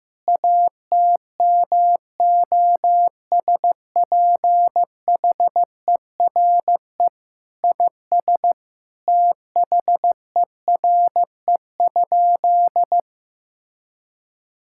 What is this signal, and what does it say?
Morse code, heard as a keyed tone, about 15 wpm, 700 Hz. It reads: ATMOSPHERE IS THERE?
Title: Sentences from Top 1600 Words 15wpm